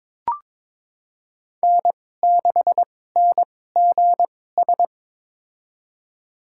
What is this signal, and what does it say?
Morse code, heard as a keyed tone, about 22 wpm, spaced as for 16 wpm, 700 Hz. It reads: N6NGS
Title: USA Call Signs - 5 Characters - 2x Character Spacing - 22wpm